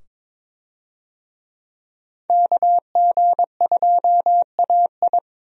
Morse code, heard as a keyed tone, about 21 wpm, 700 Hz. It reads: KG2AI
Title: USA Call Signs - 5 Characters - 22wpm